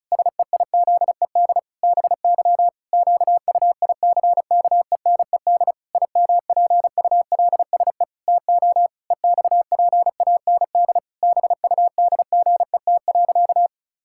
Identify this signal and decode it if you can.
Morse code, heard as a keyed tone, about 34 wpm, 700 Hz. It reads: SEIZED BY QUICKENED IMPULSE TO EXPAND BUDGET.
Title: Pangram Challenge v2 - 35wpm